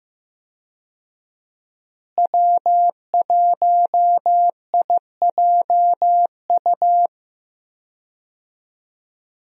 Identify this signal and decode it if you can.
Morse code, heard as a keyed tone, about 15 wpm, 700 Hz. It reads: W1IJU
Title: USA Call Signs - 5 Characters - Repeated 3x - 15wpm